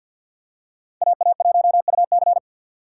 Morse code, heard as a keyed tone, about 48 wpm, 700 Hz. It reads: AA1UC
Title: K1USN SST QSO Practice - 50wpm